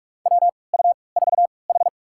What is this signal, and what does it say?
Morse code, heard as a keyed tone, about 43 wpm, spaced as for 25 wpm, 700 Hz. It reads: WU4H
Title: USA Call Signs - 4 Characters - 3x Character Spacing - 45wpm